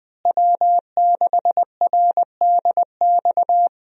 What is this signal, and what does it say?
Morse code, heard as a keyed tone, about 20 wpm, 700 Hz. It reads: W6RDX
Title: USA Call Signs - 5 Characters - Repeated 2x - 20wpm